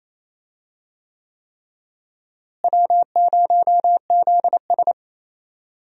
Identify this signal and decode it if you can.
Morse code, heard as a keyed tone, about 28 wpm, 700 Hz. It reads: W0ZS
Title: USA Call Signs - 4 Characters - Repeated 3x - 28wpm